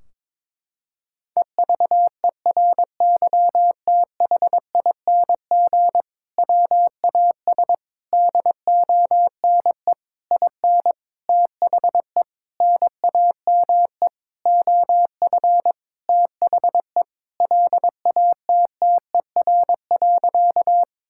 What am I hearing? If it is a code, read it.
Morse code, heard as a keyed tone, about 22 wpm, 700 Hz. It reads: EVERYTHING WAS DONE IN THE NAME OF THE LATTER.
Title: Sentences from Top 1600 Words 22wpm